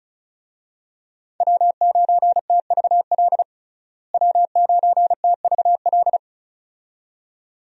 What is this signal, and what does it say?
Morse code, heard as a keyed tone, about 34 wpm, 700 Hz. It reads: W9TVL W9TVL
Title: USA Call Signs - 5 Characters - Repeated 2x - 35wpm